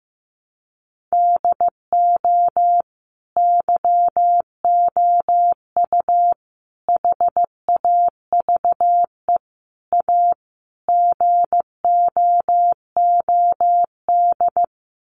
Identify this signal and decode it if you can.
Morse code, heard as a keyed tone, about 15 wpm, 700 Hz. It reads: DO YOU HAVE A GOOD
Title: Sentences from Top 1300 Words 15wpm